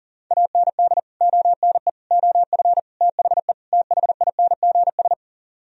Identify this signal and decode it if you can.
Morse code, heard as a keyed tone, about 54 wpm, spaced as for 47 wpm, 700 Hz. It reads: AND ONE OF THE THINGS
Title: Speed-Racing - Sets of 5 Words 40wpm